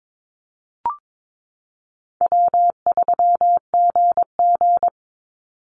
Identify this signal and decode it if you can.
Morse code, heard as a keyed tone, about 22 wpm, 700 Hz. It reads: W3GG
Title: Call Signs Repeated 3x - 22wpm